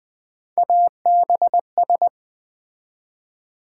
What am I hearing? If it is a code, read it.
Morse code, heard as a keyed tone, about 20 wpm, 700 Hz. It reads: ABS
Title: Build A Word - 3 Letters At A Time - 8-Letter Words - 20wpm